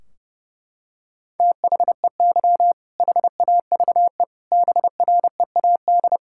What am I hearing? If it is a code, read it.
Morse code, heard as a keyed tone, about 30 wpm, 700 Hz. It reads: THEY HAVE BREAD
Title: Sentences from Top 1600 Words 30wpm